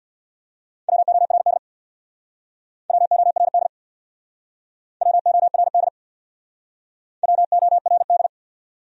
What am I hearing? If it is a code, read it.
Morse code, heard as a keyed tone, about 61 wpm, 700 Hz. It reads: WORD WORD WORD WORD
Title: Speed-Racing Top 300 Words 50wpm